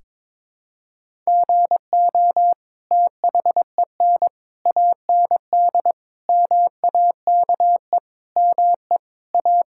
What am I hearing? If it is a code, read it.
Morse code, heard as a keyed tone, about 22 wpm, 700 Hz. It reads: GO THEN AND MAKE ME A
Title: Sentences from Top 1000 Words - Review Part 5 - 22wpm